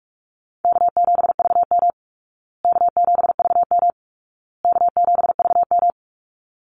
Morse code, heard as a keyed tone, about 44 wpm, 700 Hz. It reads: K7VM K7VM K7VM
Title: Call Signs Repeated 2x - 45wpm